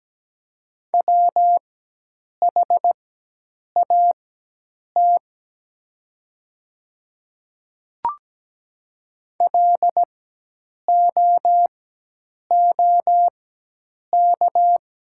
Morse code, heard as a keyed tone, about 17 wpm, spaced as for 8 wpm, 700 Hz. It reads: WHAT LOOK
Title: 4x Character Spacing - Top 200 Words - 17wpm